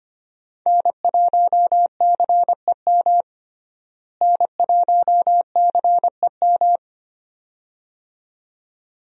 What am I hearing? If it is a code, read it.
Morse code, heard as a keyed tone, about 25 wpm, 700 Hz. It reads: N1CEM N1CEM
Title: USA Call Signs - 5 Characters - Repeated 2x - 25wpm